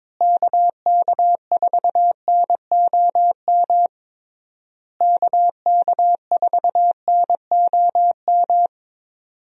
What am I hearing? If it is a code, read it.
Morse code, heard as a keyed tone, about 22 wpm, 700 Hz. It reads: KK4NOM KK4NOM
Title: USA Call Signs - 6 Characters - Repeated 3x - 22wpm